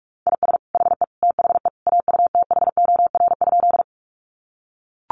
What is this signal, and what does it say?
Morse code, heard as a keyed tone, about 44 wpm, 700 Hz. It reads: IS HE THE AUTHOR?
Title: Sentences from Top 1200 Words 45wpm